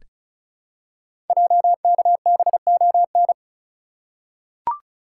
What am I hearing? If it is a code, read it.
Morse code, heard as a keyed tone, about 36 wpm, 700 Hz. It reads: JKBON
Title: Code Groups - 5 Letter:Numbers - 35wpm